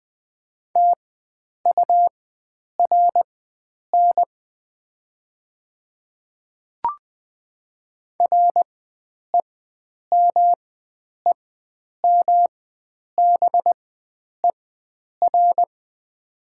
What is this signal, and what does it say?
Morse code, heard as a keyed tone, about 20 wpm, spaced as for 9 wpm, 700 Hz. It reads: TURN REMEMBER
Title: 4x Character Spacing - Top 500 Words - 20wpm